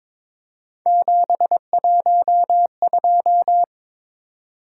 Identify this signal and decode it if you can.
Morse code, heard as a keyed tone, about 26 wpm, 700 Hz. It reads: 712
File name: Speed-Racing - Three Letter-Number Combinations - 22wpm